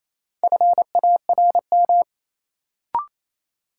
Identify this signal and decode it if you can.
Morse code, heard as a keyed tone, about 28 wpm, 700 Hz. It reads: FARM
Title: Sentences from Top 1000 Words - Review Part 2 - New Word Drill - 28wpm